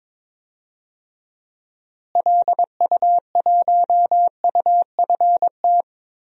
Morse code, heard as a keyed tone, about 21 wpm, 700 Hz. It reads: LU1UFT
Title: DXCC-100 Call Signs - 6 Characters - Repeated 2x - 22wpm